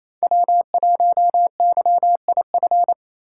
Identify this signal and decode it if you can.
Morse code, heard as a keyed tone, about 28 wpm, 700 Hz. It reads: W1YIF
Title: USA Call Signs - 6x Character Spacing - 28wpm